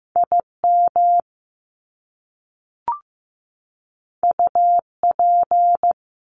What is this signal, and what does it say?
Morse code, heard as a keyed tone, about 15 wpm, 700 Hz. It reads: IM UP
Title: Top Two Letter Combinations in Words 15wpm